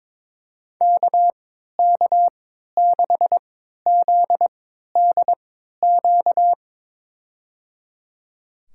Morse code, heard as a keyed tone, about 22 wpm, spaced as for 12 wpm, 700 Hz. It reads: KK6ZDQ
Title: USA Call Signs - 6 Characters - 3x Character Spacing - 22wpm